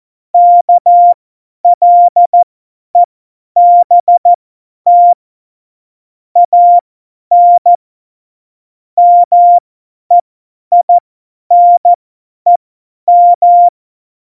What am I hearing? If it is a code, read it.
Morse code, heard as a keyed tone, about 14 wpm, spaced as for 10 wpm, 700 Hz. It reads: KLEBT AN MEINEM
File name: Morsekurs / Witze / Witz 27 March 2018, 50 F70